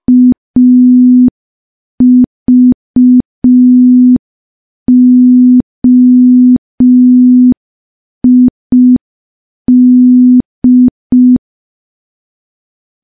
Morse code, heard as a keyed tone, about 5 wpm, 250 Hz. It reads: AVOID
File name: morse / morse-training-05WPM